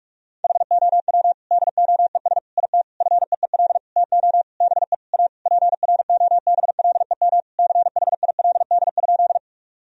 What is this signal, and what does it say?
Morse code, heard as a keyed tone, about 47 wpm, 700 Hz. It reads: HOW DOES IT FEEL TO BE A PROBLEM CHILD?
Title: Sentences from Top 1000 Words - Review Part 2 - 45wpm